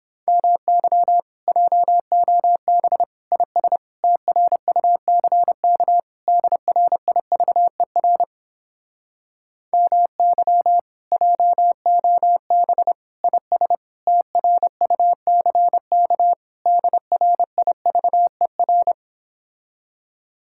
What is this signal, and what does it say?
Morse code, heard as a keyed tone, about 31 wpm, 700 Hz. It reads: MY JOB IS TRUCK DRIVER MY JOB IS TRUCK DRIVER